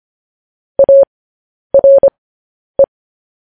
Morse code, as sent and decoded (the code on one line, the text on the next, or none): .- .-. .
ARE